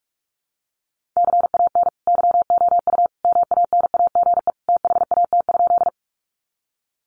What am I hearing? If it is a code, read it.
Morse code, heard as a keyed tone, about 44 wpm, 700 Hz. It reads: CAN YOU MANAGE THAT?
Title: Sentences from Top 1400 Words 45wpm